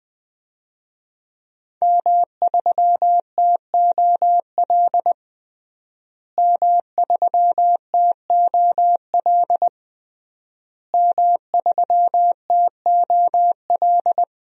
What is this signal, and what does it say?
Morse code, heard as a keyed tone, about 20 wpm, 700 Hz. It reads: M3TOL M3TOL M3TOL
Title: DXCC-100 Call Signs - 5 Characters - Repeated 3x - 20wpm